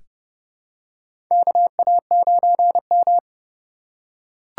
Morse code, heard as a keyed tone, about 31 wpm, 700 Hz. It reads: KA9M
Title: USA Call Signs - 4 Characters - 6x Character Spacing - 30wpm